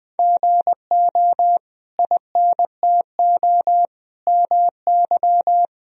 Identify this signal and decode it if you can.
Morse code, heard as a keyed tone, about 20 wpm, 700 Hz. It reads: GO INTO MY
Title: Sentences from Top 1000 Words - Review Part 1 - 20wpm